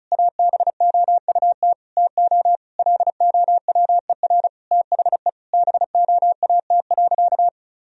Morse code, heard as a keyed tone, about 36 wpm, 700 Hz. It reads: ABOUT TO LOWER THE BOAT.
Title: Sentences from Top 1000 Words - Review Part 4 - 35wpm